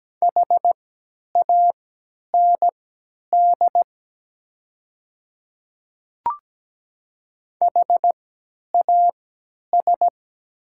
Morse code, heard as a keyed tone, about 17 wpm, spaced as for 10 wpm, 700 Hz. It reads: HAND HAS
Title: 3x Character Spacing - Top 300 Words - 17wpm